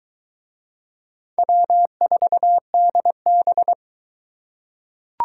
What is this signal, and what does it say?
Morse code, heard as a keyed tone, about 22 wpm, 700 Hz. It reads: W4DB